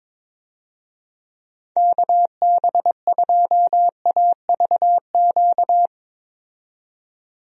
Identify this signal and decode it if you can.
Morse code, heard as a keyed tone, about 22 wpm, 700 Hz. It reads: KB2AVQ
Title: USA Call Signs - 6 Characters - 22wpm